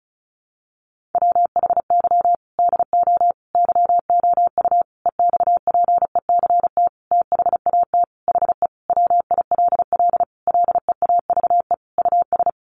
Morse code, heard as a keyed tone, about 34 wpm, 700 Hz. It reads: WHY DO YOU EXPECT THAT HE WILL LEAVE US
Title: Sentences from Top 1100 Words 35wpm